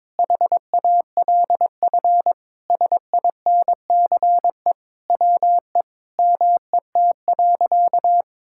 Morse code, heard as a keyed tone, about 22 wpm, 700 Hz. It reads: HALF SINCE WE MET.
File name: practice / 22wpm / Sentences with 4-Word Phrases - 22wpm